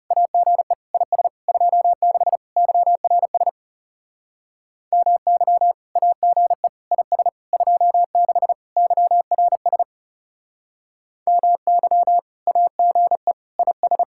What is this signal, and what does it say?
Morse code, heard as a keyed tone, about 39 wpm, 700 Hz. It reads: AGE IS 26 YRS MY AGE IS 26 YRS MY AGE IS